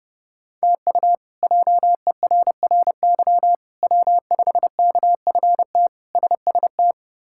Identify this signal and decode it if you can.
Morse code, heard as a keyed tone, about 31 wpm, 700 Hz. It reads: TU JERRY W5KFT SST
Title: K1USN SST QSO Practice - 30wpm